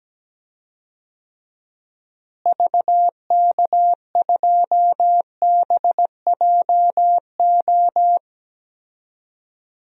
Morse code, heard as a keyed tone, about 17 wpm, 700 Hz. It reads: VK2BJO
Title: DXCC-100 Call Signs - 6 Characters - Repeated 2x - 17wpm